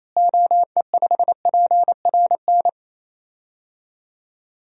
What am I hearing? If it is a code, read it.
Morse code, heard as a keyed tone, about 28 wpm, 700 Hz. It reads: OE5PRN
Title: DXCC-100 Call Signs - 6 Characters - Repeated 3x - 28wpm